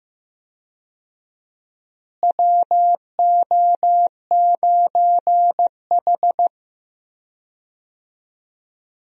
Morse code, heard as a keyed tone, about 15 wpm, 700 Hz. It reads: WO9H
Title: USA Call Signs - 4 Characters - Repeated 3x - 15wpm